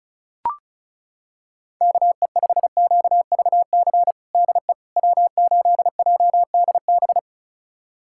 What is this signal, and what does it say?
Morse code, heard as a keyed tone, about 34 wpm, 700 Hz. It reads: KE5QVC DE W8JDB
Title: USA Callsign DE Callsign 35wpm